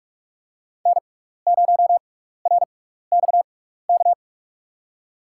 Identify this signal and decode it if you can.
Morse code, heard as a keyed tone, about 44 wpm, spaced as for 15 wpm, 700 Hz. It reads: N0RXK
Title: USA Call Signs - 6x Character Spacing - 45wpm